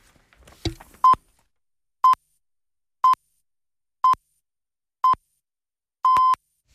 Radio time signal (time pips). It is heard from 1.0 to 6.3 s.